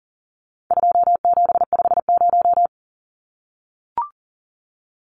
Morse code, heard as a keyed tone, about 40 wpm, 700 Hz. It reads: 2750
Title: Numbers - 4-Digit 40wpm